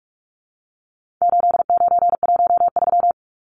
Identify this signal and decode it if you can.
Morse code, heard as a keyed tone, about 44 wpm, 700 Hz. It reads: 8913
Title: Numbers - 4-Digit 45wpm